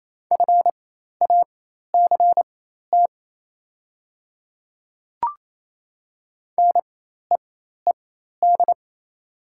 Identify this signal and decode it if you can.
Morse code, heard as a keyed tone, about 27 wpm, spaced as for 13 wpm, 700 Hz. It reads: FACT NEED
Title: 4x Character Spacing - Top 500 Words - 28wpm